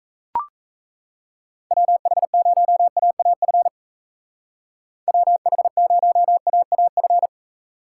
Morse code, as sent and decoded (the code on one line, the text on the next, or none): .-- .... ----- .- .- ..-. / .-- .... ----- .- .- ..-.
WH0AAF WH0AAF